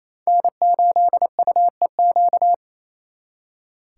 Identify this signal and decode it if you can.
Morse code, heard as a keyed tone, about 35 wpm, 700 Hz. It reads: N8UEQ